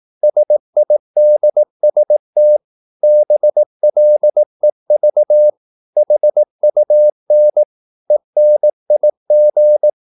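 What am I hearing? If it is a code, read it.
Morse code, heard as a keyed tone, about 18 wpm, 600 Hz. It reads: SIDST BLEV HUN ENIG
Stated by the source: Slow (QRS) Morse practice at 18 wpm, CW med Gnister.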